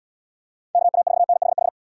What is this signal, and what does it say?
Morse code, heard as a keyed tone, about 76 wpm, 700 Hz. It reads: KA4NHH